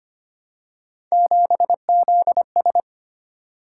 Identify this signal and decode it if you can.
Morse code, heard as a keyed tone, about 25 wpm, 700 Hz. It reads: 7ZS
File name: Three Letter-Number Combinations - 25wpm